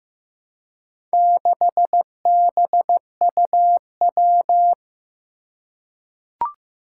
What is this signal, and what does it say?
Morse code, heard as a keyed tone, about 15 wpm, 700 Hz. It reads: 6BUW
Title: Four Challenging Letter-Number Combinations 15wpm